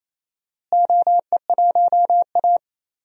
Morse code, heard as a keyed tone, about 28 wpm, 700 Hz. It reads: OE1A
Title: DXCC-100 Call Signs - 4 Characters - Repeated 2x - 28wpm